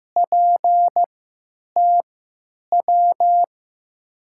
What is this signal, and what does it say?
Morse code, heard as a keyed tone, about 15 wpm, spaced as for 9 wpm, 700 Hz. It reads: PTW
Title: USA Call Signs - 6 Characters - 3x Character Spacing - 15wpm